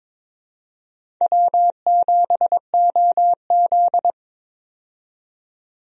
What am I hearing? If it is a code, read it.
Morse code, heard as a keyed tone, about 22 wpm, 700 Hz. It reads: W7OZ
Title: USA Call Signs - 4 Characters - 22wpm